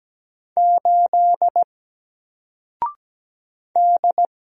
Morse code, heard as a keyed tone, about 17 wpm, 700 Hz. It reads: 8 D
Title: Single Letter-Number 17wpm